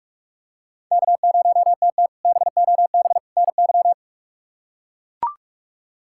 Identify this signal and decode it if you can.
Morse code, heard as a keyed tone, about 42 wpm, 700 Hz. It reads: K0TT BOB NY
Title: K1USN SST QSO Practice - 45wpm